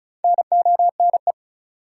Morse code, heard as a keyed tone, about 36 wpm, 700 Hz. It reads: NONE
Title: Four Letter Words - 35wpm